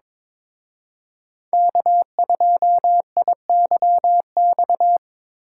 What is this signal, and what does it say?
Morse code, heard as a keyed tone, about 26 wpm, 700 Hz. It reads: K2IYX
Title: Speed Racing - USA 1x3 Call Signs 22wpm